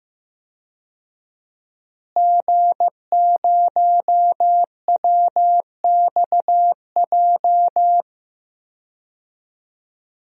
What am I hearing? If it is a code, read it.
Morse code, heard as a keyed tone, about 15 wpm, 700 Hz. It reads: G0WXJ
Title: DXCC-100 Call Signs - 5 Characters - Repeated 2x - 15wpm